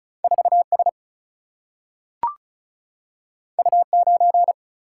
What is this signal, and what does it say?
Morse code, heard as a keyed tone, about 36 wpm, 700 Hz. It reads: 4 S U 9
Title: Single-Letter Single-Number 35wpm